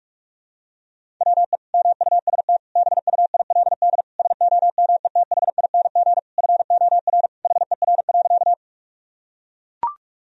Morse code, heard as a keyed tone, about 42 wpm, 700 Hz. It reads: WE MUST BUILD SOMETHING FOR HER.
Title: Sentences from Top 1000 Words - Review Part 1 - 45wpm